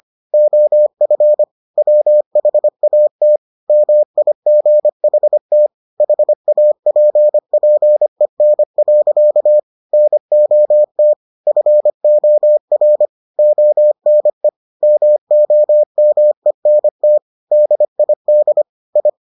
Morse code, heard as a keyed tone, about 25 wpm, 600 Hz. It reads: OF WHAT MIGHT HAPPEN. NOT FOR ONE MOMENT DID I